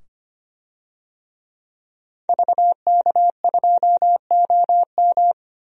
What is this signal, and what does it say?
Morse code, heard as a keyed tone, about 25 wpm, 700 Hz. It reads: VK2OM
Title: DXCC-100 Call Signs - 5 Characters - 25wpm